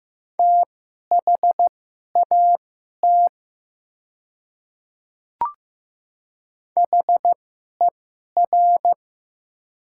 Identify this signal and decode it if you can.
Morse code, heard as a keyed tone, about 15 wpm, spaced as for 11 wpm, 700 Hz. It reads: THAT HER
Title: 2x Character Spacing - Top 100 Words - 15wpm